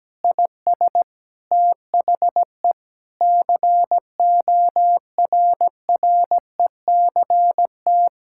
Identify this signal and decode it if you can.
Morse code, heard as a keyed tone, about 17 wpm, 700 Hz. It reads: IS THE CORRECT
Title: Sentences from Top 1000 Words - Review Part 6 - 17wpm